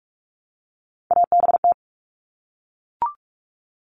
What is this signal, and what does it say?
Morse code, heard as a keyed tone, about 44 wpm, 700 Hz. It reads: ABT
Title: Abbreviations 45wpm